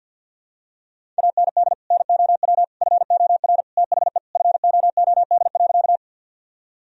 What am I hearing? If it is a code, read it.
Morse code, heard as a keyed tone, about 52 wpm, 700 Hz. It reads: AND NOW FOR THE FOOD.
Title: Sentences from Top 1000 Words - Review Part 1 - 50wpm